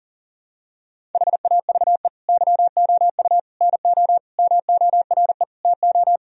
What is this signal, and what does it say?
Morse code, heard as a keyed tone, about 40 wpm, 700 Hz. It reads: HAVE YOU NO MORE TO
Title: Sentences from Top 1000 Words - Review Part 1 - 40wpm